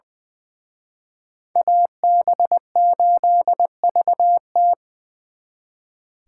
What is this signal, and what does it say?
Morse code, heard as a keyed tone, about 25 wpm, 700 Hz. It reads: AB8VT
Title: Speed Racing - USA 2x2 Call Signs 20wpm